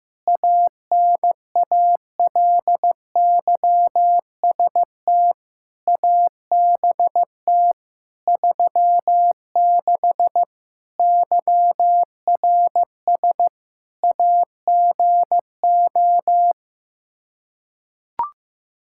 Morse code, heard as a keyed tone, about 15 wpm, 700 Hz. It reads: ANALYST ABT 36 YRS AGO
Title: Practice QSO - Exchange 3 - More Info - 15wpm